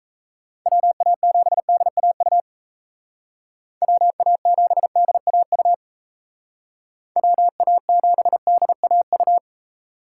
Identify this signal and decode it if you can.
Morse code, heard as a keyed tone, about 36 wpm, 700 Hz. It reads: WA7DAU WA7DAU WA7DAU